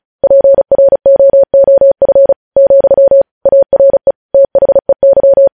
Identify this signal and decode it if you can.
Morse code, heard as a keyed tone, about 36 wpm, 550 Hz. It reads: PROOF, ARE THEY
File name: podcast / News headlines-1x-35x35